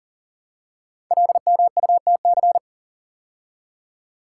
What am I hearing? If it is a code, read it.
Morse code, heard as a keyed tone, about 39 wpm, 690 Hz. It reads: LMUTC
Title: Code Groups - 5 Letters - 40wpm